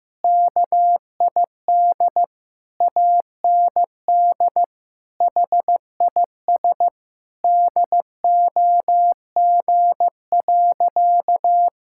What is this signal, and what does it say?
Morse code, heard as a keyed tone, about 15 wpm, 700 Hz. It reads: KID AND HIS DOG.
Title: Sentences from Top 1600 Words 15wpm